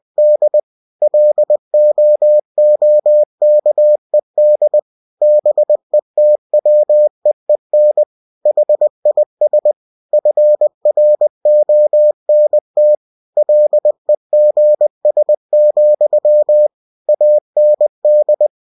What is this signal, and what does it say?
Morse code, heard as a keyed tone, about 20 wpm, 600 Hz. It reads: D LOOKED BETWEEN HIS FRONT LEGS, AND
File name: cw-books / winnie-the-pooh / winnie-20wpm-page-0027